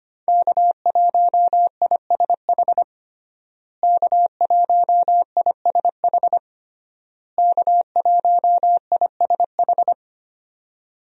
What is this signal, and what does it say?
Morse code, heard as a keyed tone, about 25 wpm, 700 Hz. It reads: K1ISH K1ISH K1ISH
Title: USA Call Signs - 5 Characters - Repeated 3x - 25wpm